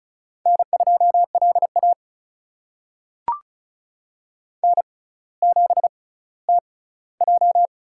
Morse code, heard as a keyed tone, about 36 wpm, spaced as for 18 wpm, 700 Hz. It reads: N2LA N7TJ
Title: USA Call Signs - 4 Characters - 6x Character Spacing - 35wpm